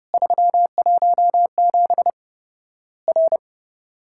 Morse code, heard as a keyed tone, about 30 wpm, 700 Hz. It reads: 317 R
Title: Worked All Europe DX Contest - QTC Exchange - 30wpm